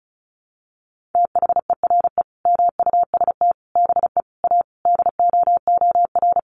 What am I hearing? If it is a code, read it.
Morse code, heard as a keyed tone, about 34 wpm, 700 Hz. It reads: THERE MUST BE A DOOR
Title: Sentences from Top 1400 Words 35wpm